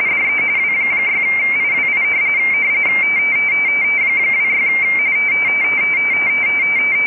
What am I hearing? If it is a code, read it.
RTTY, about 45.45 baud. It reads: DE W6IWI 26
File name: rtty / websdr recording 14086.0kHz